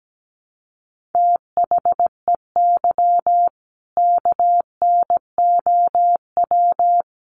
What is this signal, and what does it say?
Morse code, heard as a keyed tone, about 17 wpm, 700 Hz. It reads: THEY KNOW
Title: Sentences from Top 800 Words 17wpm